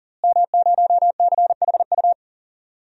Morse code, heard as a keyed tone, about 40 wpm, 700 Hz. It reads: M0CHU
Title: DXCC-100 Call Signs - 5 Characters - Repeated 2x - 40wpm